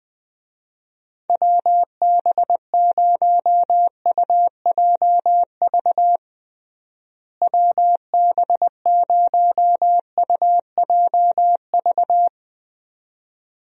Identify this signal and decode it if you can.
Morse code, heard as a keyed tone, about 21 wpm, 700 Hz. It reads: WB0UJV WB0UJV
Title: USA Call Signs - 6 Characters - Repeated 2x - 20wpm